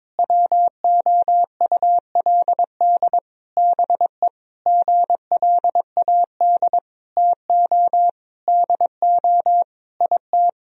Morse code, heard as a keyed tone, about 22 wpm, 700 Hz. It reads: WOULD BE GLAD TO DO IT
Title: Sentences from Top 1000 Words - Review Part 3 - 22wpm